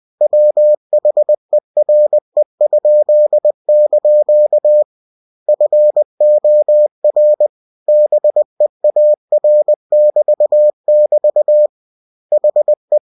Morse code, heard as a keyed tone, about 20 wpm, 600 Hz. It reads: WHERE?) FOR BEAR<BT><BT> HE
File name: cw-books / winnie-the-pooh / winnie-20wpm-page-0103